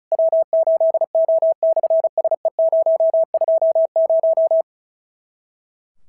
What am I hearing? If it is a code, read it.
Morse code, heard as a keyed tone, about 36 wpm, 650 Hz. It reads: W8O/SE020
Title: SOTA QSO Practice 35wpm